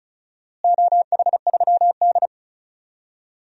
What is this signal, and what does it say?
Morse code, heard as a keyed tone, about 36 wpm, 700 Hz. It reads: OH3D
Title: DXCC-100 Call Signs - 4 Characters - Repeated 3x - 35wpm